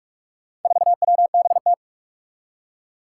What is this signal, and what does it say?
Morse code, heard as a keyed tone, about 41 wpm, 700 Hz. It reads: 4WBT
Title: Four Challenging Letter-Number Combinations 45wpm